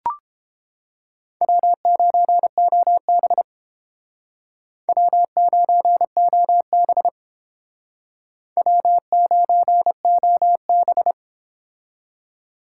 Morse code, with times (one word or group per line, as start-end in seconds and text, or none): 1.4-3.4 W9OB
4.9-7.1 W9OB
8.6-11.1 W9OB